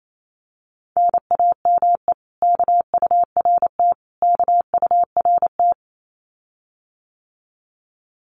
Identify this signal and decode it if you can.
Morse code, heard as a keyed tone, about 28 wpm, 700 Hz. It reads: NAME KURT KURT
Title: Common QSO Phrases Encore 28wpm